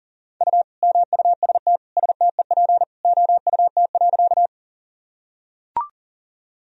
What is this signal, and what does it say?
Morse code, heard as a keyed tone, about 40 wpm, 700 Hz. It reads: U MUST STEP OUT.
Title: Sentences from Top 400 Words and Abbreviations 40wpm